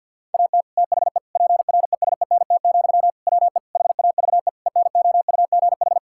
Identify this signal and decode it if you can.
Morse code, heard as a keyed tone, about 51 wpm, 690 Hz. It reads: AT THE PRESENT, WE HAVE ENOUGH
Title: Sentences from Top 1000 Words - Review Part 5 - 50wpm